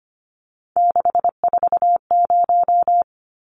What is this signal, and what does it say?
Morse code, heard as a keyed tone, about 25 wpm, 700 Hz. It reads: 640
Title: Numbers - 3-Digit 25wpm